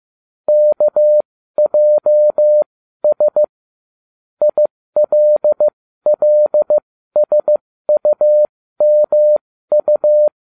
Morse code, heard as a keyed tone, about 15 wpm, spaced as for 13 wpm, 600 Hz. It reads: KJS ILLSUMU